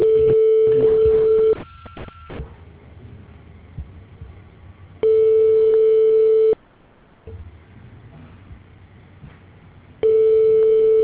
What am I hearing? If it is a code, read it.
Ringback tone (France), 440 Hz.